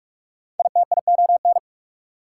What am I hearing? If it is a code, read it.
Morse code, heard as a keyed tone, about 45 wpm, 690 Hz. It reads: ITION
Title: Prefixes and Suffixes 45wpm